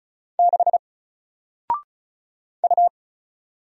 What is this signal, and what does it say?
Morse code, heard as a keyed tone, about 36 wpm, 700 Hz. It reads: 6 U
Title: Single Letter-Number 35wpm